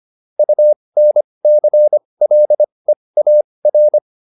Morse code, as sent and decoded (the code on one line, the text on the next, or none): ..- -. -.-. .-.. .- .-.
UNCLEAR